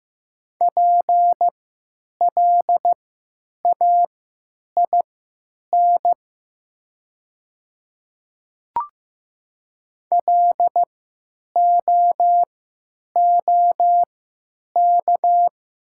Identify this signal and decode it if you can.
Morse code, heard as a keyed tone, about 15 wpm, spaced as for 8 wpm, 700 Hz. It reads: PLAIN LOOK
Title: 3x Character Spacing - Top 500 Words - 15wpm